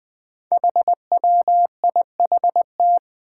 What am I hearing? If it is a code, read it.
Morse code, heard as a keyed tone, about 21 wpm, 700 Hz. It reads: HWIHT
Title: Five Challenging Letter-Number Combinations 20wpm